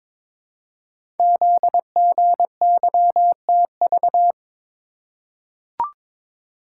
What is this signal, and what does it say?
Morse code, heard as a keyed tone, about 22 wpm, 700 Hz. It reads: ZGYTV
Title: Code Groups - 5 Letters - 22wpm